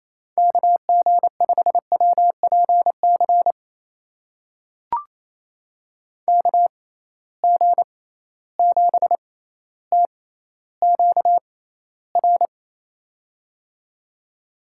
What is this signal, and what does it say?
Morse code, heard as a keyed tone, about 28 wpm, spaced as for 14 wpm, 700 Hz. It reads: KG5WPC KG7TQR
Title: USA Call Signs - 6 Characters - 6x Character Spacing - 28wpm